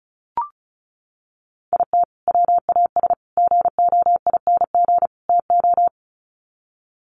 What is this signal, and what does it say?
Morse code, heard as a keyed tone, about 34 wpm, 700 Hz. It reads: IT WAS GOING TO
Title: Sets of 4 Words 35wpm